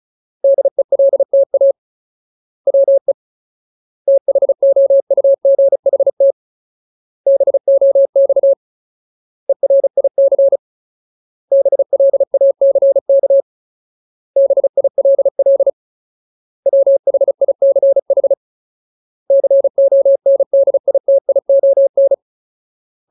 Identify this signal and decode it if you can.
Morse code, heard as a keyed tone, about 34 wpm, 550 Hz. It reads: DELTA WE THOUGHT BOX ERIC BLACK BILL WHICH CONDITION
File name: Ultimate Word List / 35WPM Ultimate